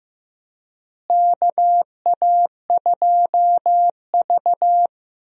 Morse code, heard as a keyed tone, about 15 wpm, 700 Hz. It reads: KA2V
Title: USA Call Signs - 4 Characters - Repeated 3x - 15wpm